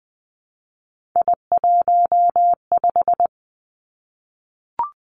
Morse code, heard as a keyed tone, about 20 wpm, 700 Hz. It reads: I15
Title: Three Challenging Letter-Number Combinations 20wpm